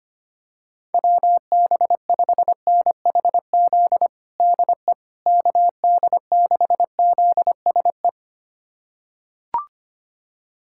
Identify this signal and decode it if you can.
Morse code, heard as a keyed tone, about 25 wpm, 700 Hz. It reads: WB5NHZ DE KD6ZSE
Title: USA Callsign DE Callsign 25wpm